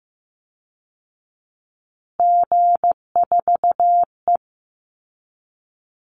Morse code, heard as a keyed tone, about 15 wpm, 700 Hz. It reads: G4E
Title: Call Signs Repeated 2x - 15wpm